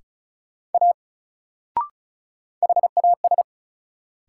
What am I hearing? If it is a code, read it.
Morse code, heard as a keyed tone, about 34 wpm, 700 Hz. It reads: A HAS
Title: Sentences from Top 100 Words - New Word Drill - 35wpm